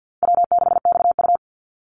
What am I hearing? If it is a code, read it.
Morse code, heard as a keyed tone, about 50 wpm, 700 Hz. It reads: W6XU
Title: Easy Call Signs Repeated 2x - 50wpm